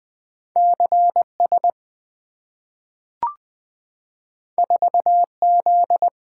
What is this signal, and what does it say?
Morse code, heard as a keyed tone, about 20 wpm, 700 Hz. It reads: CS 4Z
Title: Two Letter-Number Combinations - 20wpm